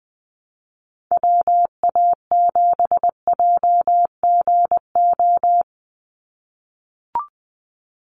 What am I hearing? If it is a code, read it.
Morse code, heard as a keyed tone, about 20 wpm, 700 Hz. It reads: WA7JGO
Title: USA Call Signs 20wpm